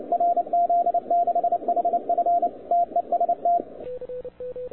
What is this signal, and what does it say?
Morse code, heard as a keyed tone, about 28 wpm, 690 Hz, with radio noise. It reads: RZ6HF TEST
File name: wwcw08 / k5zd wwcw08 0200z